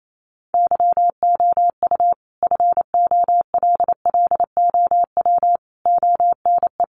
Morse code, heard as a keyed tone, about 28 wpm, 700 Hz. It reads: YOU FOLLOW ONE
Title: Sentences from Top 1300 Words 28wpm